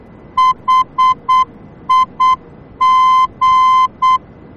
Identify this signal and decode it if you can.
Morse code, heard as a keyed tone, about 8 wpm, 1030 Hz. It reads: HIG
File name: germany / mx / HIG-D